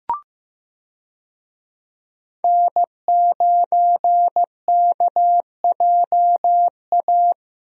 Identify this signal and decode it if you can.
Morse code, heard as a keyed tone, about 15 wpm, 700 Hz. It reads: N9KJA
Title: USA Call Signs - 5 Characters - Repeated 2x - 15wpm